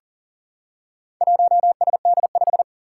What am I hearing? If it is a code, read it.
Morse code, heard as a keyed tone, about 38 wpm, 700 Hz. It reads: 1SD5
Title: Four Challenging Letter-Number Combinations 40wpm